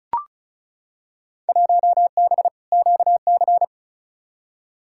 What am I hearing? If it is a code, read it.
Morse code, heard as a keyed tone, about 34 wpm, 700 Hz. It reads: 1B QC
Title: Field Day Exchanges Encore - 35wpm